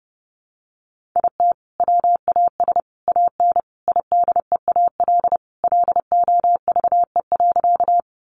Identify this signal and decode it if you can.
Morse code, heard as a keyed tone, about 30 wpm, 700 Hz. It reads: IT WAS AN IDEAL LOVE.
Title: Sentences from Top 1400 Words 30wpm